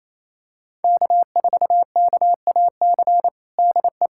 Morse code, heard as a keyed tone, about 28 wpm, 700 Hz. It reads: K4KAC DE
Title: USA Callsign DE Callsign 28wpm